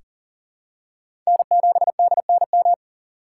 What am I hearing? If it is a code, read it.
Morse code, heard as a keyed tone, about 40 wpm, 700 Hz. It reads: N7DNM
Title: USA Call Signs - 5 Characters - 6x Character Spacing - 40wpm